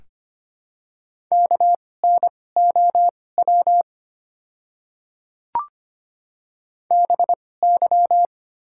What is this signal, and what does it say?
Morse code, heard as a keyed tone, about 25 wpm, spaced as for 18 wpm, 700 Hz. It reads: KNOW BY
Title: 2x Character Spacing - Top 500 Words - 25wpm